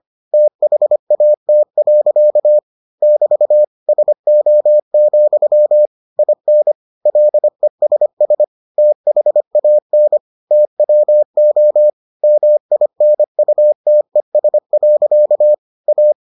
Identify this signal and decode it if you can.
Morse code, heard as a keyed tone, about 25 wpm, 600 Hz. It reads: THAT. <BT> SO, IN LESS THAN TWO MINUTES. A